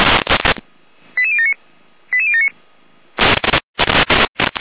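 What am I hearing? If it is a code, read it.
Five-tone selective call (ZVEI), heard as quick runs of five tones, tones of about 70 ms. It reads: 88079 88079
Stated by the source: Received on an SDR, ZVEI five-tone selective call.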